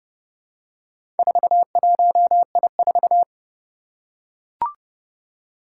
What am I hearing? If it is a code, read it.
Morse code, heard as a keyed tone, about 30 wpm, 700 Hz. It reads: 41I4
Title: Four Challenging Letter-Number Combinations 30wpm